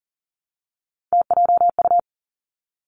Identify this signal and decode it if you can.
Morse code, heard as a keyed tone, about 42 wpm, 700 Hz. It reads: TJU
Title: Three Challenging Letter-Number Combinations 40wpm